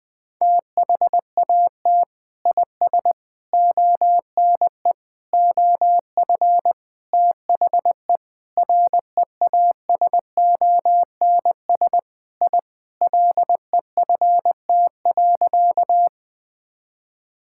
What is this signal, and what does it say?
Morse code, heard as a keyed tone, about 20 wpm, 700 Hz. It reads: THAT IS ONE OF THE REASONS I LEFT.
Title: Sentences with 4-Word Phrases - Spoken Beginning - 20wpm